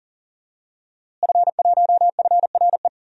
Morse code, heard as a keyed tone, about 40 wpm, 700 Hz. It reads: F1FRE
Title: DXCC-100 Call Signs - 5 Characters - Repeated 3x - 40wpm